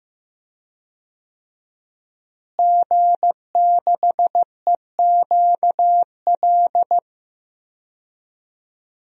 Morse code, heard as a keyed tone, about 15 wpm, 700 Hz. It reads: G6EQL
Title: DXCC-100 Call Signs - 5 Characters - Repeated 3x - 15wpm